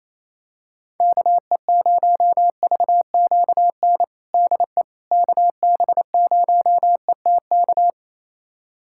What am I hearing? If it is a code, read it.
Morse code, heard as a keyed tone, about 28 wpm, 700 Hz. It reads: KE0VQN DE KB0ETK
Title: USA Callsign DE Callsign 28wpm